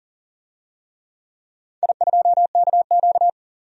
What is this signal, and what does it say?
Morse code, heard as a keyed tone, about 39 wpm, 700 Hz. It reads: I2KQ
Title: DXCC-100 Call Signs - 4 Characters - 40wpm